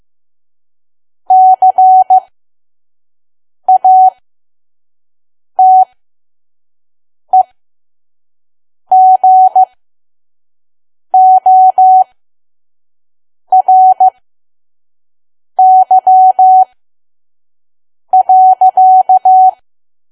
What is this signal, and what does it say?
Morse code, heard as a keyed tone, about 15 wpm, spaced as for 5 wpm, 750 Hz. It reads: CATEGORY.
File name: Archive / 5 WPM files / PM